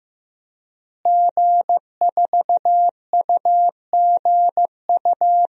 Morse code, heard as a keyed tone, about 15 wpm, 700 Hz. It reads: G4UGU
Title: DXCC-100 Call Signs - 5 Characters - Repeated 2x - 15wpm